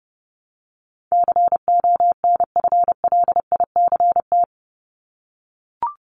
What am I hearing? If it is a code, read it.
Morse code, heard as a keyed tone, about 30 wpm, 700 Hz. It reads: CONFLICT
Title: Top 1001-1250 Words 30wpm